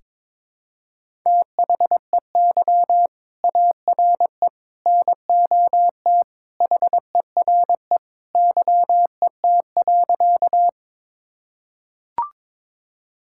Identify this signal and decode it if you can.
Morse code, heard as a keyed tone, about 22 wpm, 700 Hz. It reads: THEY ARE NOT HERE YET.
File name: practice / 22wpm / Sentences with 2-Word Phrases - 22wpm